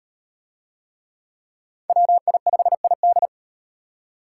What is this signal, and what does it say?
Morse code, heard as a keyed tone, about 38 wpm, 700 Hz. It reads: WI5ID